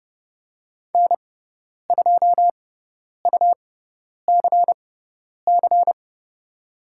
Morse code, heard as a keyed tone, about 30 wpm, spaced as for 10 wpm, 700 Hz. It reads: N2UCC
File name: Lesson 17g - Letter u - Call Signs 30wpm